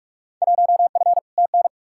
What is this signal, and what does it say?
Morse code, heard as a keyed tone, about 47 wpm, 700 Hz. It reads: 1F TN
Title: Field Day Exchanges - 45wpm